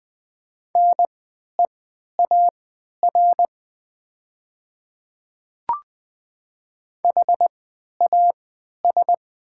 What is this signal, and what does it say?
Morse code, heard as a keyed tone, about 20 wpm, spaced as for 11 wpm, 700 Hz. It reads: NEAR HAS
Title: 3x Character Spacing - Top 200 Words - 20wpm